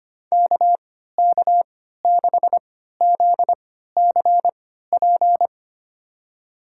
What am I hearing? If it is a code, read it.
Morse code, heard as a keyed tone, about 25 wpm, spaced as for 14 wpm, 700 Hz. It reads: KK6ZCP
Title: USA Call Signs - 6 Characters - 3x Character Spacing - 25wpm